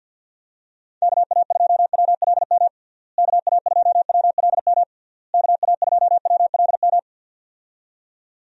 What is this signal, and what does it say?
Morse code, heard as a keyed tone, about 51 wpm, 700 Hz. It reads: KA2WLM KA2WLM KA2WLM
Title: USA Call Signs - 6 Characters - Repeated 3x - 50wpm